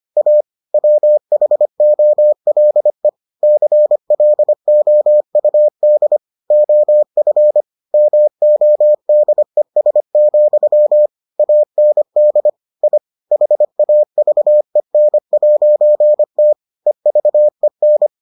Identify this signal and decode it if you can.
Morse code, heard as a keyed tone, about 25 wpm, 600 Hz. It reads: A WHOLE CLOUD OF MODES, AND I HAVEN'T EVEN